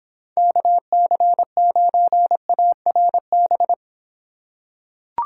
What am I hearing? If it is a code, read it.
Morse code, heard as a keyed tone, about 26 wpm, 700 Hz. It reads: KC9ARB